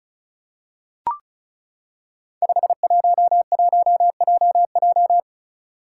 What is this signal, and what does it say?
Morse code, heard as a keyed tone, about 36 wpm, 700 Hz. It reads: 511JJ
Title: Five Challenging Letter-Number Combinations 35wpm